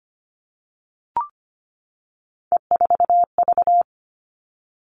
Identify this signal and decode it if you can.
Morse code, heard as a keyed tone, about 25 wpm, 700 Hz. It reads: E4V
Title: All Cut Numbers - 3-Digit 25wpm